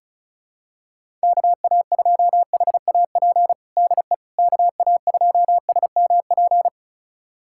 Morse code, heard as a keyed tone, about 34 wpm, 700 Hz. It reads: KA2HAP DE KA2SMP
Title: USA Callsign DE Callsign - 6 Characters - 35wpm